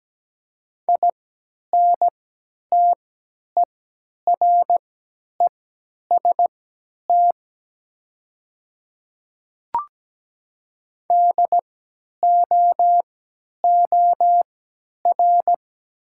Morse code, heard as a keyed tone, about 17 wpm, spaced as for 10 wpm, 700 Hz. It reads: INTEREST DOOR
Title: 3x Character Spacing - Top 500 Words - 17wpm